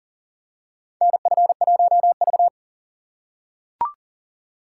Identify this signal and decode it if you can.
Morse code, heard as a keyed tone, about 40 wpm, 700 Hz. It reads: NF1V
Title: USA Call Signs - 2x Character Spacing - 40wpm